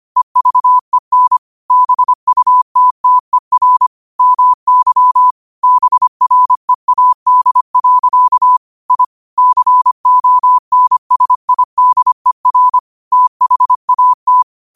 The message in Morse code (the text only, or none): EVEN BUTTER MY BREAD. I CONSIDER THAT